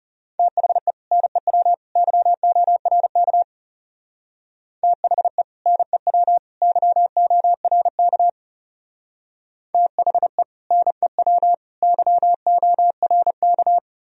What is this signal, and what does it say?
Morse code, heard as a keyed tone, about 34 wpm, 700 Hz. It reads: THE NEW YORK THE NEW YORK THE NEW YORK
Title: Speed-Racing - Sets of 3 Words 30wpm